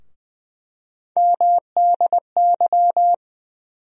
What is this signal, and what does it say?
Morse code, heard as a keyed tone, about 20 wpm, 700 Hz. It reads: MDY
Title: USA Call Sign Postfixes - 3 Letters 20wpm